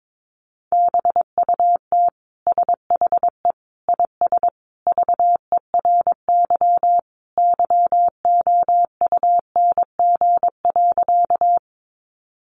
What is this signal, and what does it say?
Morse code, heard as a keyed tone, about 22 wpm, 700 Hz. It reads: BUT SHE IS VERY YOUNG.
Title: Sentences from Top 400 Words 22wpm